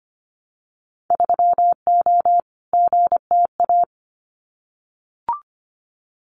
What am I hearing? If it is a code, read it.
Morse code, heard as a keyed tone, about 25 wpm, 700 Hz. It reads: 3O GTA
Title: Winter Field Day Exchanges Encore 25wpm